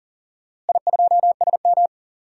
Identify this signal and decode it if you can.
Morse code, heard as a keyed tone, about 40 wpm, 700 Hz. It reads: I2SM
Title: DXCC-100 Call Signs - 4 Characters - Repeated 3x - 40wpm